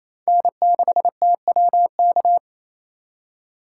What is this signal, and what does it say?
Morse code, heard as a keyed tone, about 35 wpm, 700 Hz. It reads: N6TWK